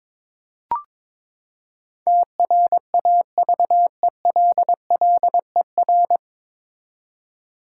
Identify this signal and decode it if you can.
Morse code, heard as a keyed tone, about 22 wpm, 700 Hz. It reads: TRAVELLER
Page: Morse Code Ninja practice 22wpm